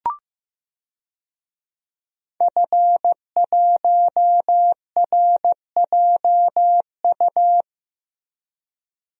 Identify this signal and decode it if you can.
Morse code, heard as a keyed tone, about 15 wpm, 700 Hz. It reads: F1RJU
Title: DXCC-100 Call Signs - 5 Characters - 15wpm